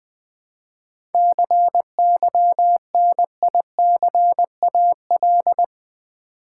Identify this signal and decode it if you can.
Morse code, heard as a keyed tone, about 20 wpm, 700 Hz. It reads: CYNICAL
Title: 7 Letter Words - First 3 Letters Given - 20wpm